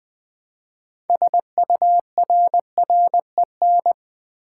Morse code, heard as a keyed tone, about 20 wpm, 700 Hz. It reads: SURREN
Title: Build A Word - 3 Letters At A Time - 9-Letter Words - 20wpm